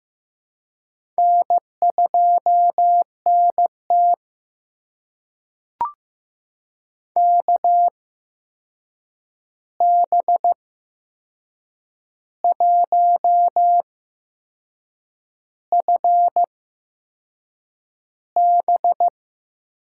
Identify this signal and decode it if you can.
Morse code, heard as a keyed tone, about 15 wpm, spaced as for 6 wpm, 700 Hz. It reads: N2NT KB1FB
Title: USA Call Signs - 8x Character Spacing - 15wpm